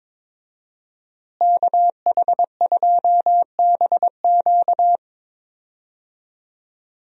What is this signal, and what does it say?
Morse code, heard as a keyed tone, about 22 wpm, 700 Hz. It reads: KH2BQ
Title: USA Call Signs - 5 Characters - Repeated 2x - 22wpm